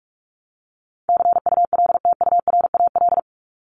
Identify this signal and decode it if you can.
Morse code, heard as a keyed tone, about 44 wpm, 700 Hz. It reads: CULTURAL
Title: Top 1251-1500 Words 45wpm